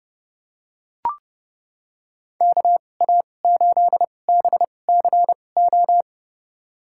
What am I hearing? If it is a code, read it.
Morse code, heard as a keyed tone, about 30 wpm, spaced as for 22 wpm, 700 Hz. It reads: KA8BCO
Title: USA Call Signs - 2x Character Spacing - 30wpm